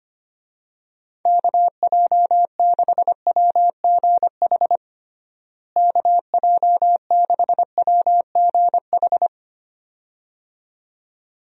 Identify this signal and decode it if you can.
Morse code, heard as a keyed tone, about 25 wpm, 700 Hz. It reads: KJ6WGH KJ6WGH
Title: USA Call Signs - 6 Characters - Repeated 2x - 25wpm